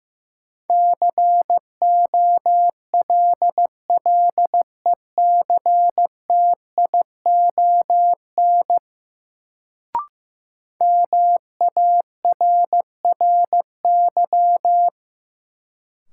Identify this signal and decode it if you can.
Morse code, heard as a keyed tone, about 15 wpm, 700 Hz. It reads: COLLECTION MARRY
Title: Sentences from Top 1200 Words - New Word Drill - 15wpm